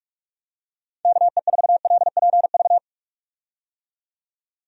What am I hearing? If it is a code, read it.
Morse code, heard as a keyed tone, about 46 wpm, 700 Hz. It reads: KE4LPV
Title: USA Call Signs - 6 Characters - 45wpm